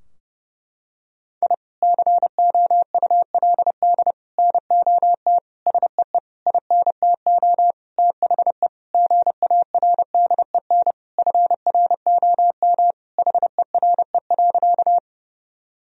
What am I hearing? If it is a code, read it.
Morse code, heard as a keyed tone, about 30 wpm, 700 Hz. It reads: I COULD NOT SEE INTO THE GARDEN FROM HERE.
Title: Sentences from Top 1000 Words - Review Part 4 - 30wpm